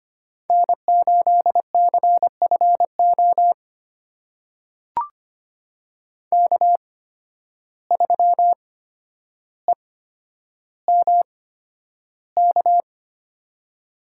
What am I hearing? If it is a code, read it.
Morse code, heard as a keyed tone, about 25 wpm, spaced as for 11 wpm, 700 Hz. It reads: N8CFO K3EMK
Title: USA Call Signs - 5 Characters - 8x Character Spacing - 25wpm